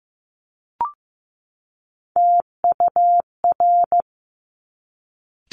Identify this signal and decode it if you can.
Morse code, heard as a keyed tone, about 15 wpm, 700 Hz. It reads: TUR
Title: Top Three Letter Combinations in Words 15wpm